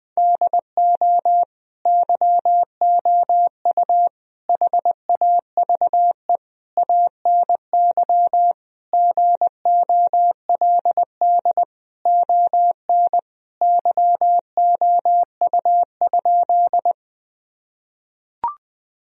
Morse code, heard as a keyed tone, about 21 wpm, 700 Hz. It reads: DO YOU HAVE ANY GOLD ON YOU?
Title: Sentences from Top 1000 Words - Review Part 3 - 20wpm